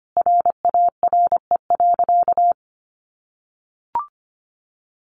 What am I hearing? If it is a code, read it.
Morse code, heard as a keyed tone, about 25 wpm, 700 Hz. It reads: RARE.
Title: Sentences from Top 1100 Words 25wpm